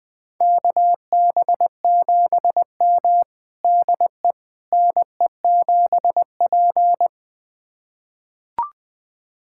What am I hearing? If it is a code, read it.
Morse code, heard as a keyed tone, about 21 wpm, 700 Hz. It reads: KB7M DE NE7P
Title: USA Callsign DE Callsign - 4 Characters - 20wpm